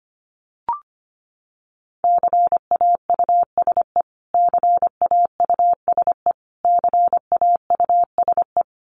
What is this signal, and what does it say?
Morse code, heard as a keyed tone, about 25 wpm, 700 Hz. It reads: CAUSE CAUSE CAUSE
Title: Top 500 Words Repeated 25wpm